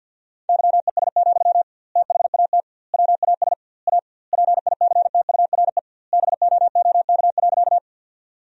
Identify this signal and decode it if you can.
Morse code, heard as a keyed tone, about 50 wpm, 700 Hz. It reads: YES, THAT WAS A PICTURE BOOK.
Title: 2x Word-Spacing - Sentences from Top 500 Words - 50wpm